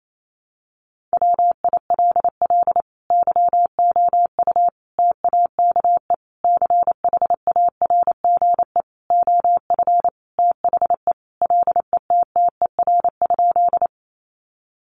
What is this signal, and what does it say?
Morse code, heard as a keyed tone, about 28 wpm, 700 Hz. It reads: WILL YOU TAKE CHARGE OF THE LETTER?
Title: Sentences from Top 900 Words 28wpm